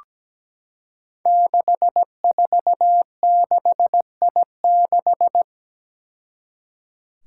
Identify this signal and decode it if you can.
Morse code, heard as a keyed tone, about 17 wpm, 700 Hz. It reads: 646I6
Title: Five Challenging Letter-Number Combinations 17wpm